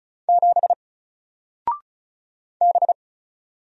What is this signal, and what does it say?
Morse code, heard as a keyed tone, about 34 wpm, 700 Hz. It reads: 7 B
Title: Single Character (Letter:Number:Punctuation:Prosign) 35wpm